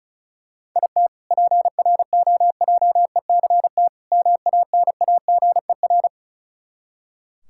Morse code, heard as a keyed tone, about 36 wpm, 700 Hz. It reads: IT PROJECT MANAGER
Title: K1USN SST QSO Practice 35wpm